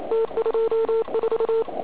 Morse code, heard as a keyed tone, about 36 wpm, 450 Hz, with noise. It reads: T24
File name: Archiv / CUC106 / cuc106 PILEUP